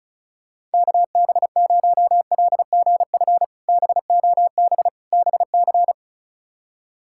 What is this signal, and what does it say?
Morse code, heard as a keyed tone, about 36 wpm, 700 Hz. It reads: KB0LGF BOB BC
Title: K1USN SST QSO Practice - 35wpm